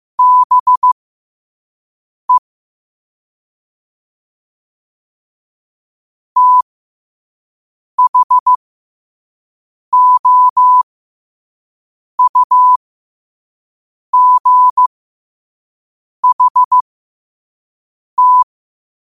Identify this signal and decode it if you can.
Morse code, heard as a keyed tone, about 15 wpm, spaced as for 5 wpm, 1000 Hz. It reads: BE THOUGHT
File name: qotdfiles / 5wpm